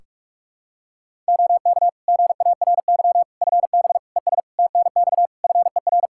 Morse code, heard as a keyed tone, about 46 wpm, 700 Hz. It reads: OK GARY FB ES TNX FER
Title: Practice QSO - Exchange 2 - Info - 45wpm